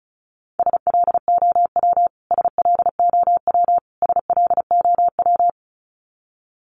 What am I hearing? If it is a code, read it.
Morse code, heard as a keyed tone, about 34 wpm, 700 Hz. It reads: SLOW SLOW SLOW
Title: Top 500 Words Repeated 35wpm